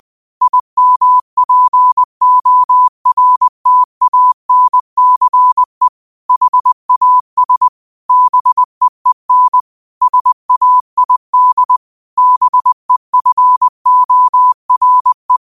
Morse code, heard as a keyed tone, about 20 wpm, 1000 Hz. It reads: IMPORTANCE HAS BEEN SAID BEFORE